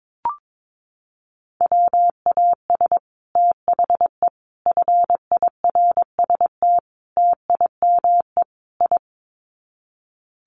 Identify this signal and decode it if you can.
Morse code, heard as a keyed tone, about 22 wpm, 700 Hz. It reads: WAS THE FIRST TIME I
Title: Sets of 5 Words Encore 22wpm